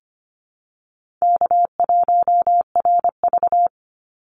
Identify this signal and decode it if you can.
Morse code, heard as a keyed tone, about 25 wpm, 700 Hz. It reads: K1RV
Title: Call Signs Repeated 3x - 25wpm